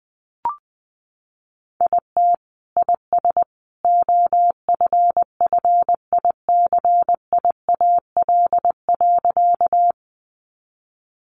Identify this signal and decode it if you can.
Morse code, heard as a keyed tone, about 19 wpm, 700 Hz. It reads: IT IS OFFICIAL.
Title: Sentences from Top 1000 Words 20wpm